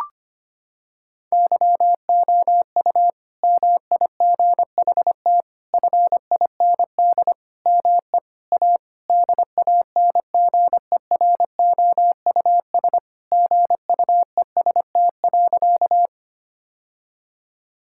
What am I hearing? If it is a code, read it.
Morse code, heard as a keyed tone, about 25 wpm, 700 Hz. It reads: YOU MIGHT FIND ME A DANGEROUS GUEST.
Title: Sentences from Top 1600 Words 25wpm